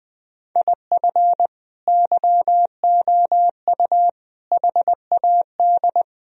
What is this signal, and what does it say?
Morse code, heard as a keyed tone, about 20 wpm, 700 Hz. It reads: IF YOU HAD
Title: Sentences from Top 1000 Words - Review Part 3 - 20wpm